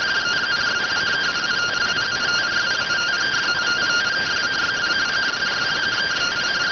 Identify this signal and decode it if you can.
Two-tone teleprinter signal, about 170 Hz shift.